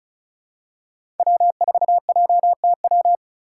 Morse code, heard as a keyed tone, about 34 wpm, 700 Hz. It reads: W4JTW
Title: USA Call Signs - 5 Characters - 4x Character Spacing - 35wpm